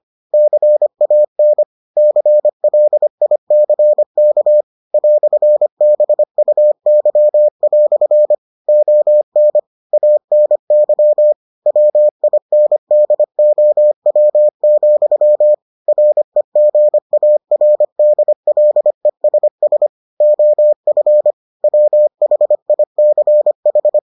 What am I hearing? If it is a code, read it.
Morse code, heard as a keyed tone, about 25 wpm, 600 Hz. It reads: CAN CLICK "BUY" ON ANY WINDOW, REGARDLESS OF WHICH